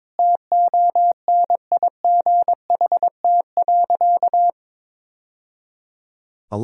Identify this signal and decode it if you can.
Morse code, heard as a keyed tone, about 22 wpm, 700 Hz. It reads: TONIGHT.
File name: practice / 22wpm / Sentences with 2-Word Phrases - 22wpm